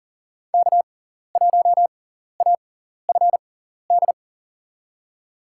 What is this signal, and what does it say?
Morse code, heard as a keyed tone, about 40 wpm, spaced as for 14 wpm, 700 Hz. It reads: K1AFD
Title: USA Call Signs - 5 Characters - 6x Character Spacing - 40wpm